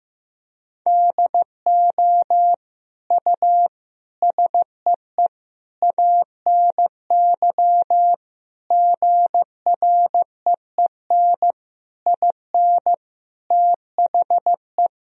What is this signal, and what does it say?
Morse code, heard as a keyed tone, about 15 wpm, 700 Hz. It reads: DO U SEE ANY GREEN IN THE
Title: Sentences from Top 500 Words and Abbreviations 15wpm